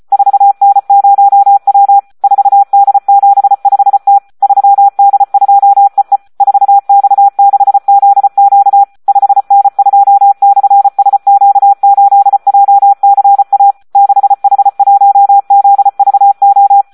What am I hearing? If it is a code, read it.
Morse code, heard as a keyed tone, about 34 wpm, 800 Hz. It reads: 4N0W 4D75T 3D2EE 4X6ZQ 5N2/SQ8JCA 6H1ZVO
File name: Audio practica CW / SD 1 34